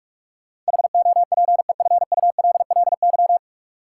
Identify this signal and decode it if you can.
Morse code, heard as a keyed tone, about 46 wpm, 700 Hz. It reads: HOPEFULLY
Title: Sentences from Top 1500 Words - New Word Drill - 45wpm